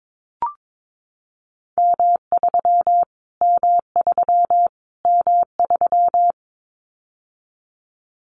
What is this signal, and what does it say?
Morse code, heard as a keyed tone, about 22 wpm, 700 Hz. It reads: M3 M3 M3
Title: Callsign Prefixes for Top 25 DXCC - Repeated 3x - 22wpm